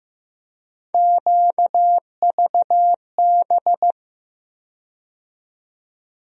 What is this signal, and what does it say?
Morse code, heard as a keyed tone, about 15 wpm, 700 Hz. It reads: QVB
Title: USA Call Sign Postfixes - 3 Letters 15wpm